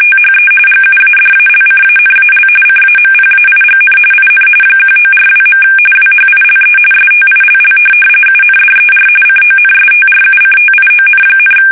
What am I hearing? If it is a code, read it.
RTTY, about 45.45 baud. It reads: THE GUADALCANAL CAMPAIGN IN 1942. WHILE STILL UNDER CONSTRUCTION,
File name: fsk850 / USS-Massachusetts